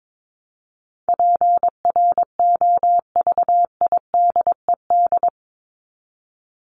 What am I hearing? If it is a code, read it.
Morse code, heard as a keyed tone, about 22 wpm, 700 Hz. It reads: PROVIDED
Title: Top 1751-2000 Words 22wpm